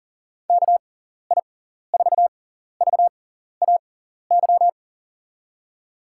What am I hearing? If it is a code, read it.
Morse code, heard as a keyed tone, about 37 wpm, spaced as for 14 wpm, 700 Hz. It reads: KI4VAY
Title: USA Call Signs - 6x Character Spacing - 40wpm